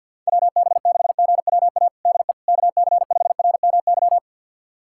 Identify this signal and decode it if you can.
Morse code, heard as a keyed tone, about 50 wpm, 700 Hz. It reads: WB6GWA DE KC5RMY
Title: USA Callsign DE Callsign 50wpm